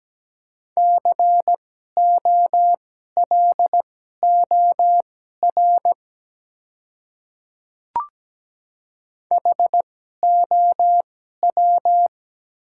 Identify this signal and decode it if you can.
Morse code, heard as a keyed tone, about 17 wpm, spaced as for 12 wpm, 700 Hz. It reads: COLOR HOW
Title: 2x Character Spacing - Top 300 Words - 17wpm